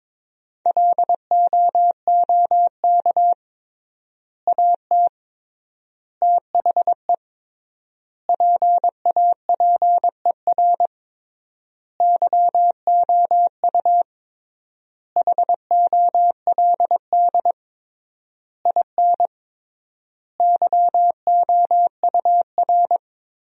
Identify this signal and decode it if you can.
Morse code, heard as a keyed tone, about 22 wpm, 700 Hz. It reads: LOOK AT THE PAPER YOU HOLD IN YOUR
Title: 3x Word-Spacing - Sentences from Top 400 Words - 22wpm